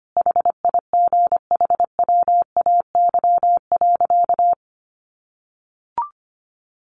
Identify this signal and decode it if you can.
Morse code, heard as a keyed tone, about 25 wpm, 700 Hz. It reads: HIGHWAY.
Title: Sentences from Top 1400 Words 25wpm